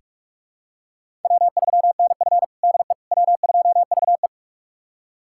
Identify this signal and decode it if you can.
Morse code, heard as a keyed tone, about 43 wpm, 700 Hz. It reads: W3NF DE W2VE
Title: USA Callsign DE Callsign - 4 Characters - 45wpm